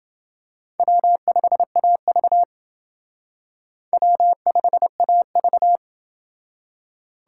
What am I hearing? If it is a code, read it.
Morse code, heard as a keyed tone, about 29 wpm, 700 Hz. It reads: W5AV W5AV